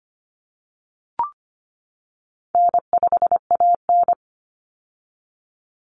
Morse code, heard as a keyed tone, about 25 wpm, 700 Hz. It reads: N5AN
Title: Easy Call Signs - 25wpm